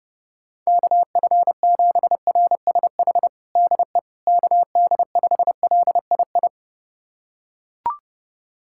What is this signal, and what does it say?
Morse code, heard as a keyed tone, about 30 wpm, 700 Hz. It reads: KF7RSH DE KD5LII
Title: USA Callsign DE Callsign 30wpm